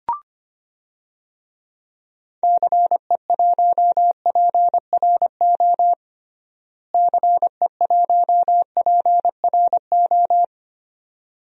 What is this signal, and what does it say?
Morse code, heard as a keyed tone, about 25 wpm, 700 Hz. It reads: CE1PRO CE1PRO
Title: DXCC-100 Call Signs - 6 Characters - Repeated 2x - 25wpm